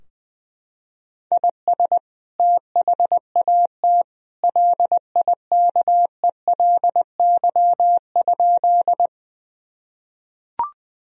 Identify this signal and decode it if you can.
Morse code, heard as a keyed tone, about 20 wpm, 700 Hz. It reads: IS THAT LIKELY?
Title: Sentences from Top 1000 Words - Review Part 4 - 20wpm